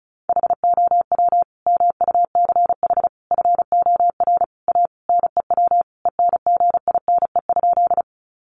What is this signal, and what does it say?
Morse code, heard as a keyed tone, about 34 wpm, 700 Hz. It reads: HOW MUCH FOR A NEW ENGINE?
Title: Sentences from Top 1500 Words 35wpm